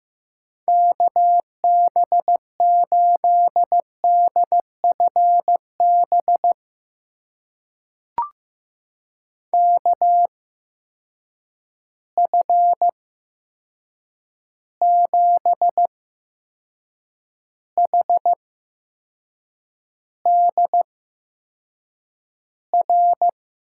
Morse code, heard as a keyed tone, about 15 wpm, spaced as for 6 wpm, 700 Hz. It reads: KB8DFB KF7HDR
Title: USA Call Signs - 6 Characters - 8x Character Spacing - 15wpm